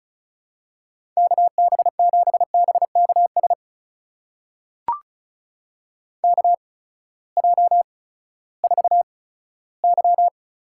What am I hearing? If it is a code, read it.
Morse code, heard as a keyed tone, about 36 wpm, spaced as for 18 wpm, 700 Hz. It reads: KB7BKS KJ4Y
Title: USA Call Signs - 6 Characters - 8x Character Spacing - 35wpm